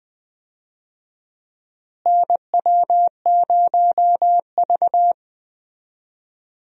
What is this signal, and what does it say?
Morse code, heard as a keyed tone, about 20 wpm, 700 Hz. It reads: NW0V
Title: USA Call Signs - 4 Characters - Repeated 2x - 20wpm